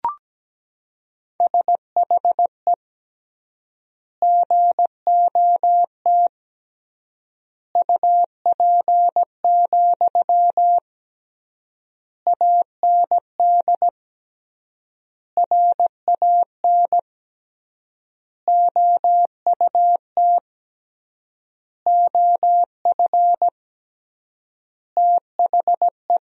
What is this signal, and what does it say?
Morse code, heard as a keyed tone, about 17 wpm, 700 Hz. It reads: SHE GOT UP, AND RAN OUT OF THE
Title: 3x Word-Spacing - Sentences from Top 500 Words - 17wpm